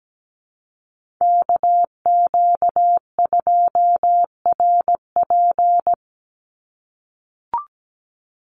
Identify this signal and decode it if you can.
Morse code, heard as a keyed tone, about 17 wpm, 700 Hz. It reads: KQ2RP
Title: NA SOTA Chaser Callsigns 17wpm